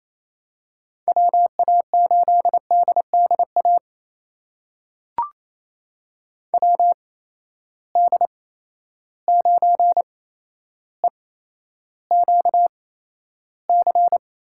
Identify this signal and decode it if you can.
Morse code, heard as a keyed tone, about 27 wpm, spaced as for 12 wpm, 700 Hz. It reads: WA8DDA WD9EQC
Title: USA Call Signs - 6 Characters - 8x Character Spacing - 28wpm